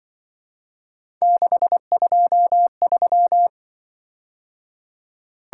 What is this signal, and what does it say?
Morse code, heard as a keyed tone, about 25 wpm, 700 Hz. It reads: 623
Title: Speed-Racing - Three Letter-Number Combinations - 20wpm